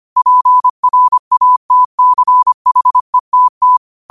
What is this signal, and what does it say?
Morse code, heard as a keyed tone, about 25 wpm, 1000 Hz. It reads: PRATCHETT
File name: qotdfiles / 25wpm